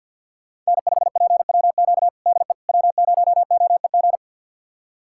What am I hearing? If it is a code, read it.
Morse code, heard as a keyed tone, about 51 wpm, 700 Hz. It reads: N5PWQ DE W0OEG
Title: USA Callsign DE Callsign - 5 Characters - 50wpm